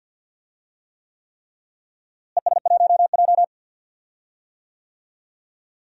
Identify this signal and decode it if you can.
Morse code, heard as a keyed tone, about 53 wpm, 700 Hz. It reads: ES1J
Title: DXCC-100 Call Signs - 4 Characters - Repeated 3x - 50wpm